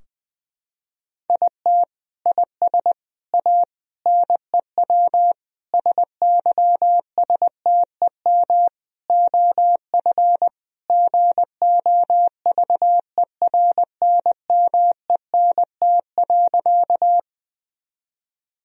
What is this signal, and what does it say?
Morse code, heard as a keyed tone, about 21 wpm, 700 Hz. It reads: IT IS A NEW SYSTEM OF GOVERNMENT.
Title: Sentences from Top 1000 Words - Review Part 3 - 20wpm